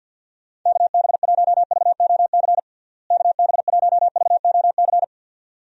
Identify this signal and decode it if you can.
Morse code, heard as a keyed tone, about 47 wpm, 700 Hz. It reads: KB1VOC KB1VOC
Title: USA Call Signs - 6 Characters - Repeated 2x - 50wpm